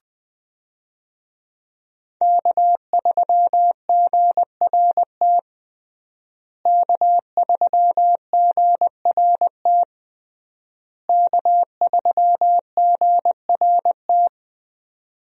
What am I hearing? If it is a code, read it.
Morse code, heard as a keyed tone, about 20 wpm, 700 Hz. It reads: K3GRT K3GRT K3GRT
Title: USA Call Signs - 5 Characters - Repeated 3x - 20wpm